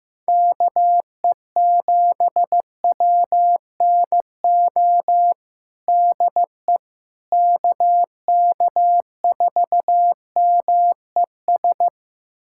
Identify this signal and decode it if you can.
Morse code, heard as a keyed tone, about 15 wpm, 700 Hz. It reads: KE7WNO DE KK4MES
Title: USA Callsign DE Callsign - 6 Characters - 15wpm